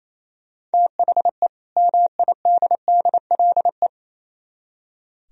Morse code, heard as a keyed tone, about 34 wpm, 700 Hz. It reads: THE MIDDLE
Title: Speed-Racing - Sets of 2 Words 28wpm